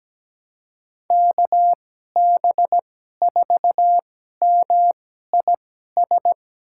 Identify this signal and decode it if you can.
Morse code, heard as a keyed tone, about 17 wpm, spaced as for 12 wpm, 700 Hz. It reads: KB4MIS
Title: USA Call Signs - 6 Characters - 2x Character Spacing - 17wpm